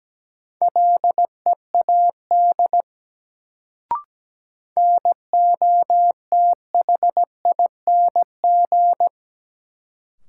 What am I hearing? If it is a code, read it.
Morse code, heard as a keyed tone, about 17 wpm, 700 Hz. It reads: LEAD NOTHING
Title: Sentences from Top 500 Words - New Word Drill - 17wpm